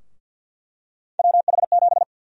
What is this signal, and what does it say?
Morse code, heard as a keyed tone, about 49 wpm, 700 Hz. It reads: WH7
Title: USA Call Sign Prefixes - Repeated 2x - 50wpm